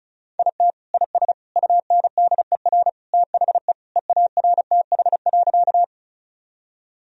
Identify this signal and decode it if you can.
Morse code, heard as a keyed tone, about 34 wpm, 700 Hz. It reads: IT IS UNDER THE EARTH.
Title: Sentences from Top 1000 Words - Review Part 1 - 35wpm